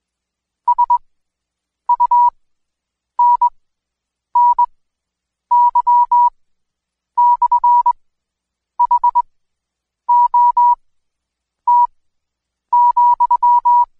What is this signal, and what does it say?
Morse code, heard as a keyed tone, about 22 wpm, spaced as for 8 wpm, 970 Hz. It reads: SUNNY/HOT,
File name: Code Quick / AudioDownld2 / Seg114-Day19-DayDrill 19